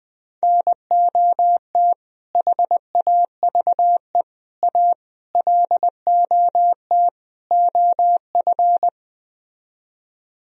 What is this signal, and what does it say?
Morse code, heard as a keyed tone, about 20 wpm, 700 Hz. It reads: NOT HAVE A LOT OF
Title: Speed-Racing - Sets of 5 Words 17wpm